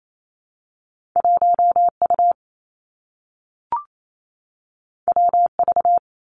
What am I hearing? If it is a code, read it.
Morse code, heard as a keyed tone, about 28 wpm, 700 Hz. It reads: 1 U W V
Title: Two Challenging Letter-Number Combinations 28wpm